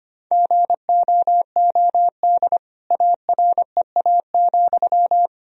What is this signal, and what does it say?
Morse code, heard as a keyed tone, about 25 wpm, 700 Hz. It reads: GOOD AREA,
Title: Sentences from Top 1000 Words - Review Part 2 - 25wpm